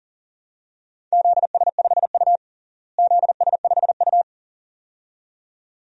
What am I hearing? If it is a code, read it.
Morse code, heard as a keyed tone, about 40 wpm, 700 Hz. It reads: ZS5U ZS5U
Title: DXCC-100 Call Signs - 4 Characters - Repeated 2x - 40wpm